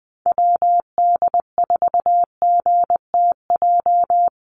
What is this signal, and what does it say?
Morse code, heard as a keyed tone, about 20 wpm, 700 Hz. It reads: WD4GTJ
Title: USA Call Signs Repeated 2x 20wpm